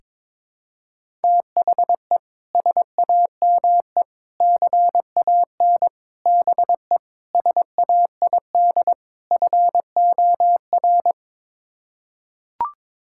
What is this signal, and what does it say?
Morse code, heard as a keyed tone, about 22 wpm, 700 Hz. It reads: THE SAME CAN BE SAID FOR
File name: practice / 22wpm / Sets of 6 Words - 22wpm